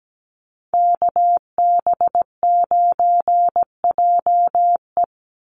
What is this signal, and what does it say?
Morse code, heard as a keyed tone, about 17 wpm, 700 Hz. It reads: KB9JE
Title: USA Call Signs 17wpm